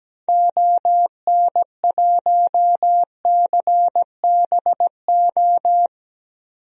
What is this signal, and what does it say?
Morse code, heard as a keyed tone, about 17 wpm, 700 Hz. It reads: ON1CBO
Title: DXCC-100 Call Signs - 6 Characters - 17wpm